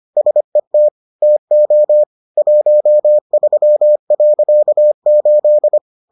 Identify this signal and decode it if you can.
Morse code, heard as a keyed tone, about 25 wpm, 600 Hz. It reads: SET TO 13.8